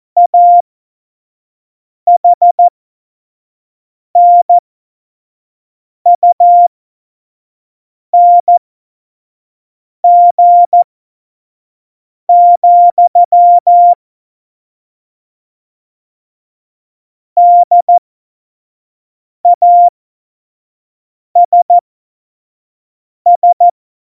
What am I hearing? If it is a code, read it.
Morse code, heard as a keyed tone, about 14 wpm, spaced as for 5 wpm, 700 Hz. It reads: AHNUNG, DASS
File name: Morsekurs / Witze / Witz 04. Februar 2020, 25 F70